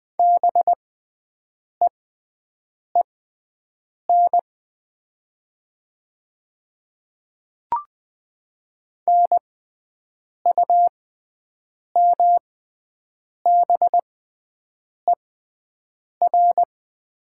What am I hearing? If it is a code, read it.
Morse code, heard as a keyed tone, about 20 wpm, spaced as for 7 wpm, 700 Hz. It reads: BEEN NUMBER
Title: 6x Character Spacing - Top 100 Words - 20wpm